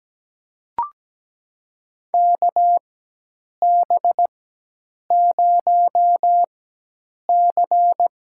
Morse code, heard as a keyed tone, about 17 wpm, spaced as for 8 wpm, 700 Hz. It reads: KB0C
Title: USA Call Signs - 6 Characters - 4x Character Spacing - 17wpm